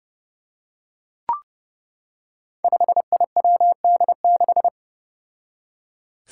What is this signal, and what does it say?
Morse code, heard as a keyed tone, about 30 wpm, 700 Hz. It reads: 5IWD6
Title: Five Challenging Letter-Number Combinations 30wpm